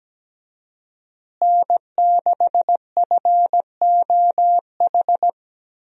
Morse code, heard as a keyed tone, about 17 wpm, 700 Hz. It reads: N6FOH
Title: USA Call Signs - 5 Characters - Repeated 3x - 17wpm